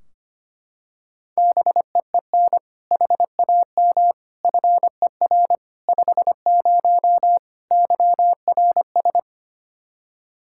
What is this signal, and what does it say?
Morse code, heard as a keyed tone, about 25 wpm, 700 Hz. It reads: BEEN HAM FER 50 YRS
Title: Practice QSO - Exchange 3 - More Info - 25wpm